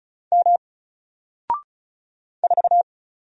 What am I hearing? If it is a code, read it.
Morse code, heard as a keyed tone, about 36 wpm, 700 Hz. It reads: M 4
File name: Single-Letter-Number-35wpm